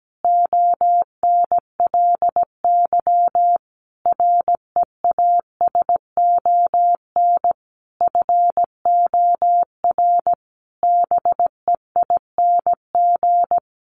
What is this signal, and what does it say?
Morse code, heard as a keyed tone, about 17 wpm, 700 Hz. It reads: ONLY REASON FOR BEING
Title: Sentences from Top 700 Words 17wpm